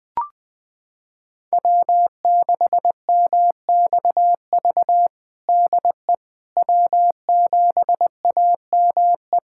Morse code, heard as a keyed tone, about 19 wpm, 700 Hz. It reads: W6MXV DE W7AME
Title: USA Callsign DE Callsign - 5 Characters - 20wpm